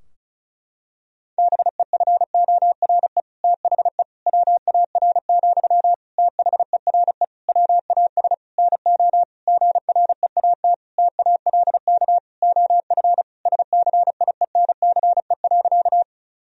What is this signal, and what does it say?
Morse code, heard as a keyed tone, about 34 wpm, 700 Hz. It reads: BEFORE THE WAR, THERE WAS NO GREAT TALK OF SCIENCE.
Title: Sentences from Top 1000 Words - Review Part 2 - 35wpm